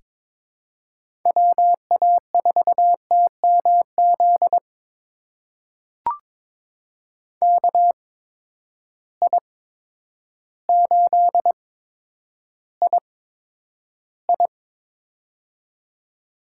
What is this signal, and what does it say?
Morse code, heard as a keyed tone, about 22 wpm, spaced as for 10 wpm, 700 Hz. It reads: WA4TMZ KI8II
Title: USA Call Signs - 8x Character Spacing - 22wpm